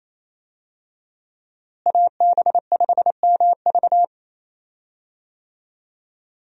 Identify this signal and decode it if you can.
Morse code, heard as a keyed tone, about 28 wpm, 700 Hz. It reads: AB5MV
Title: USA Call Signs - 5 Characters - Repeated 3x - 28wpm